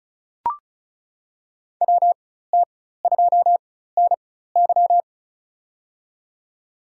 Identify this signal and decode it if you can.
Morse code, heard as a keyed tone, about 34 wpm, spaced as for 16 wpm, 700 Hz. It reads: WT2NY
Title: USA Call Signs - 5 Characters - 4x Character Spacing - 35wpm